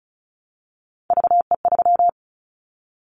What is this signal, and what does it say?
Morse code, heard as a keyed tone, about 36 wpm, 700 Hz. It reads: VE3
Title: Callsign Prefixes for Top 25 DXCC - 35wpm